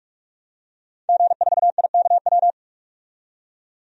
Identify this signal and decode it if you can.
Morse code, heard as a keyed tone, about 46 wpm, 700 Hz. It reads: G4IKW
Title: DXCC-100 Call Signs - 5 Characters - 45wpm